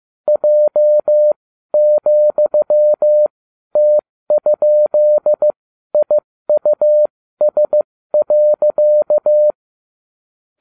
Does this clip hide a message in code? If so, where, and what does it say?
Yes: Morse code from 0.3 to 9.5 s, heard as a keyed tone, about 15 wpm, spaced as for 12 wpm, 600 Hz. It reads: J,T?IUS.